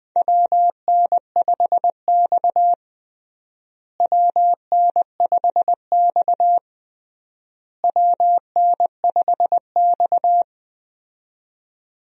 Morse code, heard as a keyed tone, about 20 wpm, 700 Hz. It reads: WN5X WN5X WN5X
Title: USA Call Signs - 4 Characters - Repeated 3x - 20wpm